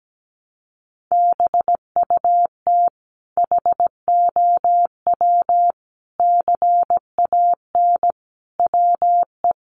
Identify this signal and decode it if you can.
Morse code, heard as a keyed tone, about 17 wpm, 700 Hz. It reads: BUT HOW CAN WE
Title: Sentences from Top 1300 Words 17wpm